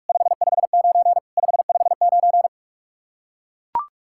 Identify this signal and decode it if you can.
Morse code, heard as a keyed tone, about 41 wpm, 700 Hz. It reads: 559 559
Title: K1USN SST QSO Practice 45wpm